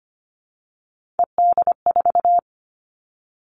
Morse code, heard as a keyed tone, about 25 wpm, 700 Hz. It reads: ED4
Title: All Cut Numbers - 3-Digit 25wpm